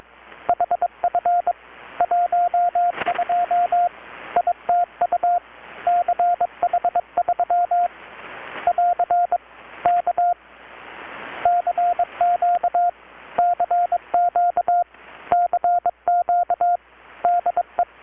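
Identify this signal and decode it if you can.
Morse code, heard as a keyed tone, about 22 wpm, 720 Hz, with radio noise. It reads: HF 12 ITU CH3 <AR> K CQ CQ CQ DE
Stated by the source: Morse from KPH's Night of Nights XX maritime radiotelegraphy event, received in Chiapas, Mexico about 3,800 km away on a Kenwood R-600 with a grounded 10 m long-wire antenna.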